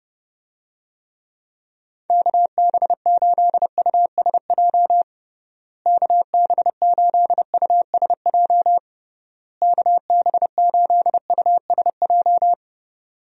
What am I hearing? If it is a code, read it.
Morse code, heard as a keyed tone, about 30 wpm, 700 Hz. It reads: KB8USJ KB8USJ KB8USJ
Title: USA Call Signs - 6 Characters - Repeated 3x - 30wpm